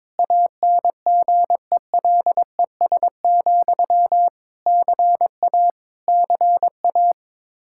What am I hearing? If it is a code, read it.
Morse code, heard as a keyed tone, about 21 wpm, 700 Hz. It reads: ANGELES, CA CA
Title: Practice QSO - Exchange 1 - Beginning - 22wpm